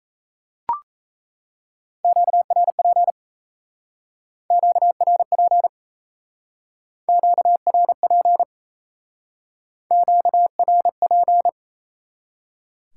Morse code, heard as a keyed tone, about 38 wpm, 700 Hz. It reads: QRP QRP QRP QRP